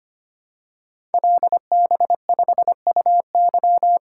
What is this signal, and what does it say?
Morse code, heard as a keyed tone, about 25 wpm, 700 Hz. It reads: LB5UY
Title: Code Groups - 5 Letter:Numbers - 25wpm